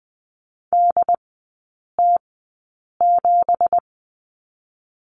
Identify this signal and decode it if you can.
Morse code, heard as a keyed tone, about 20 wpm, spaced as for 8 wpm, 700 Hz. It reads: DT7
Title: 166 Lesson 27e - Number 7 - Three Characters 20wpm